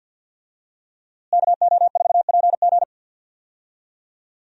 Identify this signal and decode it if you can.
Morse code, heard as a keyed tone, about 49 wpm, 700 Hz. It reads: KO4PG
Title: USA Call Signs - 5 Characters - Repeated 3x - 50wpm